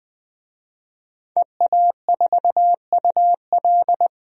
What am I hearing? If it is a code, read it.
Morse code, heard as a keyed tone, about 20 wpm, 700 Hz. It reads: EA4UL
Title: DXCC-100 Call Signs - 5 Characters - 20wpm